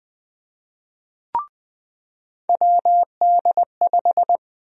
Morse code, heard as a keyed tone, about 20 wpm, 700 Hz. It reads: WD5
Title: USA Call Sign Prefixes - 20wpm